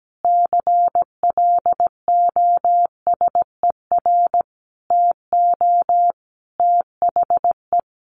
Morse code, heard as a keyed tone, about 17 wpm, 700 Hz. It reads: CLOSER TO THE
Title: Additional Easy Phrases 17wpm